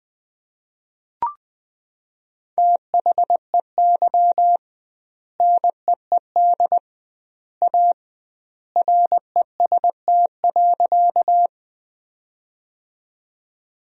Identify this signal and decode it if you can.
Morse code, heard as a keyed tone, about 20 wpm, 700 Hz. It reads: THEY NEED A REST.
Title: 2x Word-Spacing - Sentences from Top 500 Words - 20wpm